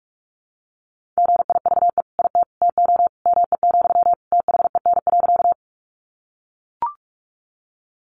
Morse code, heard as a keyed tone, about 44 wpm, 700 Hz. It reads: GIVE IT TO ME, THEN.
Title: Sentences from Top 200 Words 45wpm